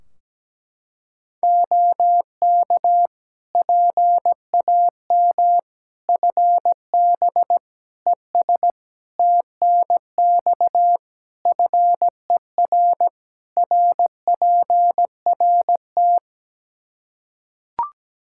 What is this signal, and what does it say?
Morse code, heard as a keyed tone, about 17 wpm, 700 Hz. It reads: OK PAM FB ES TNX FER RPRT
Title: K1USN SST QSO Practice 17wpm